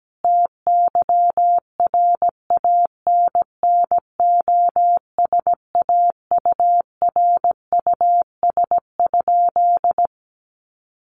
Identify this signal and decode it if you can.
Morse code, heard as a keyed tone, about 17 wpm, 700 Hz. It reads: TYRANNOSAURUS?
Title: A Sound Of Thunder 17wpm